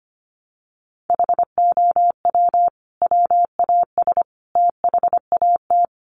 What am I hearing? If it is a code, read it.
Morse code, heard as a keyed tone, about 25 wpm, 700 Hz. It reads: HOW WAS THAT
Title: Sentences from Top 500 Words 25wpm